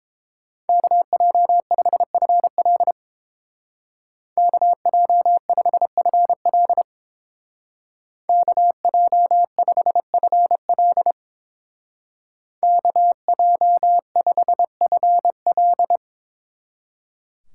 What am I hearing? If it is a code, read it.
Morse code, heard as a keyed tone, about 29 wpm, 700 Hz. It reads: KJ5FL KJ5FL KJ5FL KJ5FL